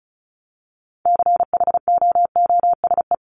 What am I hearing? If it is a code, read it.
Morse code, heard as a keyed tone, about 34 wpm, 700 Hz. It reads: CHOOSE
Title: Top 501-750 Words 35wpm